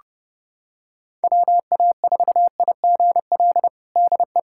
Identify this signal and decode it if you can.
Morse code, heard as a keyed tone, about 32 wpm, 700 Hz. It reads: WA4IGL DE
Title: USA Callsign DE Callsign - 6 Characters - 30wpm